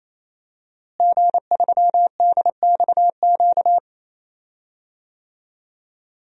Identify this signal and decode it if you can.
Morse code, heard as a keyed tone, about 28 wpm, 700 Hz. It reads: G3DXQ
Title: DXCC-100 Call Signs - 5 Characters - Repeated 3x - 28wpm